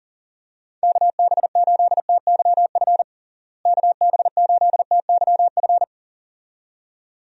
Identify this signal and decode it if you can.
Morse code, heard as a keyed tone, about 40 wpm, 700 Hz. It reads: KB8TYF KB8TYF
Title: USA Call Signs - 6 Characters - Repeated 2x - 40wpm